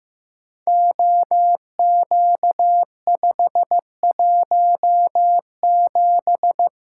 Morse code, heard as a keyed tone, about 15 wpm, 700 Hz. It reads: OQ517
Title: Code Groups - 5 Letter:Numbers - 15wpm